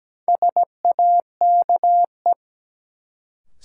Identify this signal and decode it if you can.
Morse code, heard as a keyed tone, about 17 wpm, 700 Hz. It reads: SAKE
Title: Sets of 4 Words - New Word Drill - 17wpm